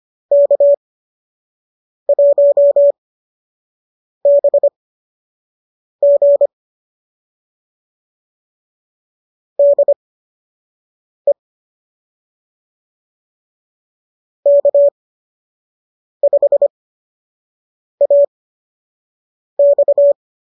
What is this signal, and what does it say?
Morse code, heard as a keyed tone, about 25 wpm, spaced as for 6 wpm, 570 Hz. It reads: K1BG DE K5AX